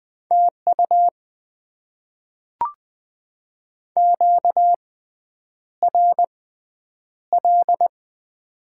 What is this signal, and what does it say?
Morse code, heard as a keyed tone, about 20 wpm, spaced as for 9 wpm, 700 Hz. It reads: TU QRL
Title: Top 20 QSO Elements - 6x Character Spacing - 20wpm